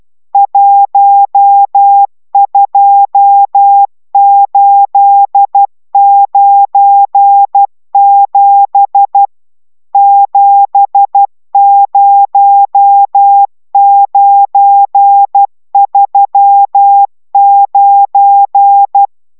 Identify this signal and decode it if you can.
Morse code, heard as a keyed tone, about 12 wpm, 800 Hz. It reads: 12897 70939